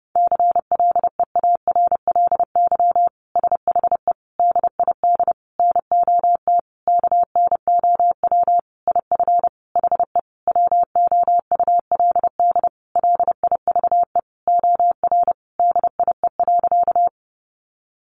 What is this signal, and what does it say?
Morse code, heard as a keyed tone, about 30 wpm, 700 Hz. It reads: CLEARLY SHE DID NOT KNOW IF HE WOULD LIVE OR DIE.
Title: Sentences from Top 1300 Words 30wpm